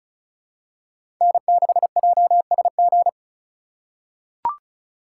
Morse code, heard as a keyed tone, about 36 wpm, 700 Hz. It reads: N6JSG
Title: USA Call Signs - 4x Character Spacing - 35wpm